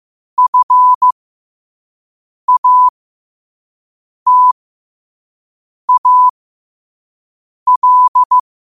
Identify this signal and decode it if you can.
Morse code, heard as a keyed tone, about 15 wpm, spaced as for 5 wpm, 1000 Hz. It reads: FATAL
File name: qotdfiles / 5wpm